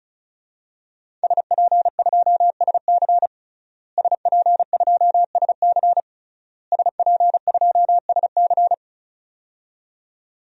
Morse code, heard as a keyed tone, about 36 wpm, 700 Hz. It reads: SP2SC SP2SC SP2SC
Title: DXCC-100 Call Signs - 5 Characters - Repeated 3x - 35wpm